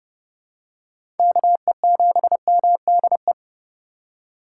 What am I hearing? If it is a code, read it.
Morse code, heard as a keyed tone, about 30 wpm, 700 Hz. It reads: KE7MDE
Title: USA Call Signs - 4x Character Spacing - 30wpm